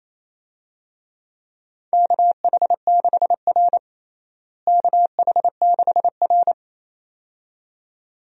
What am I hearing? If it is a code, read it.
Morse code, heard as a keyed tone, about 28 wpm, 700 Hz. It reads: KH6R KH6R
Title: DXCC-100 Call Signs - 4 Characters - Repeated 2x - 28wpm